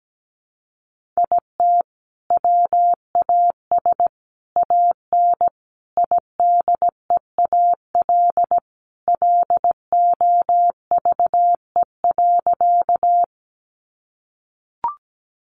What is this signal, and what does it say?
Morse code, heard as a keyed tone, about 17 wpm, 700 Hz. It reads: IT WAS AN IDEAL LOVE.
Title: Sentences from Top 1400 Words 17wpm